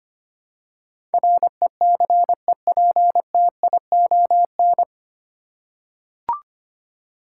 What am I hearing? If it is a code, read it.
Morse code, heard as a keyed tone, about 25 wpm, 700 Hz. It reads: RECEPTION
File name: practice / 25wpm / 9 Letter Words - First 3 Letters Given - 25wpm